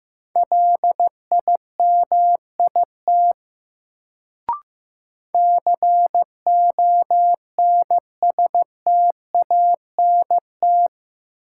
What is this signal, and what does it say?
Morse code, heard as a keyed tone, about 15 wpm, 700 Hz. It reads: LIMIT CONSTANT
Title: Sentences from Top 1500 Words - New Word Drill Alt - 15wpm